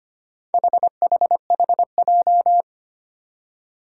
Morse code, heard as a keyed tone, about 25 wpm, 700 Hz. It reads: HHHJ
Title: Four Challenging Letter-Number Combinations 25wpm